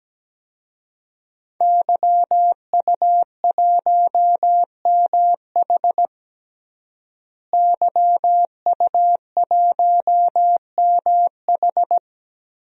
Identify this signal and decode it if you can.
Morse code, heard as a keyed tone, about 17 wpm, 700 Hz. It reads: YU1MH YU1MH
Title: DXCC-100 Call Signs - 5 Characters - Repeated 3x - 17wpm